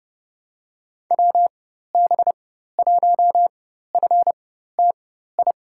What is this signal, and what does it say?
Morse code, heard as a keyed tone, about 30 wpm, spaced as for 14 wpm, 700 Hz. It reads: WB1FTI
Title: USA Call Signs - 6 Characters - 4x Character Spacing - 30wpm